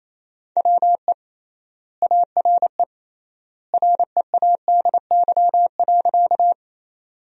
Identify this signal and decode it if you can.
Morse code, heard as a keyed tone, about 28 wpm, 700 Hz. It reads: WE ARE READY.
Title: 3x Word-Spacing - Sentences from Top 400 Words - 28wpm